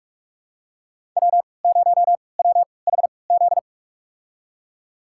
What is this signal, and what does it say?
Morse code, heard as a keyed tone, about 46 wpm, spaced as for 26 wpm, 700 Hz. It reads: W0WHZ
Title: USA Call Signs - 3x Character Spacing - 45wpm